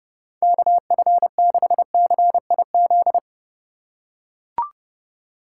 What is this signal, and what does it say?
Morse code, heard as a keyed tone, about 31 wpm, 700 Hz. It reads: KF6CIZ
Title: USA Call Signs - 6 Characters - 2x Character Spacing - 30wpm